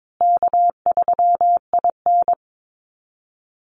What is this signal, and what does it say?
Morse code, heard as a keyed tone, about 22 wpm, 700 Hz. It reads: K3IN
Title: Easy Call Signs Repeated 3x - 22wpm